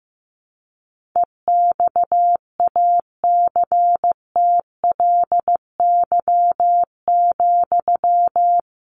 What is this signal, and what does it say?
Morse code, heard as a keyed tone, about 15 wpm, 700 Hz. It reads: EXACTLY,
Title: Sentences from Top 1500 Words 15wpm